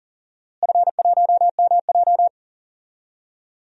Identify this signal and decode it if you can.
Morse code, heard as a keyed tone, about 40 wpm, 700 Hz. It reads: F1MJ
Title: DXCC-100 Call Signs - 4 Characters - Repeated 3x - 40wpm